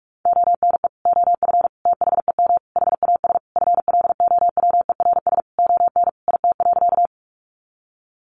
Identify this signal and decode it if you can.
Morse code, heard as a keyed tone, about 44 wpm, 700 Hz. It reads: ONE OF THEM HAS FLOWERS ON IT.
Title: Additional Easy Phrases 45wpm